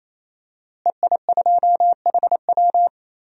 Morse code, heard as a keyed tone, about 28 wpm, 700 Hz. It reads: EI2HW
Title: DXCC-100 Call Signs - 5 Characters - Repeated 3x - 28wpm